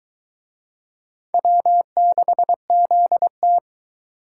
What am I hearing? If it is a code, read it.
Morse code, heard as a keyed tone, about 22 wpm, 700 Hz. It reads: W6ZT